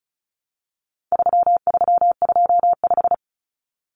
Morse code, heard as a keyed tone, about 34 wpm, 700 Hz. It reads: 3325
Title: Common Cut Numbers - 4-Digit 35wpm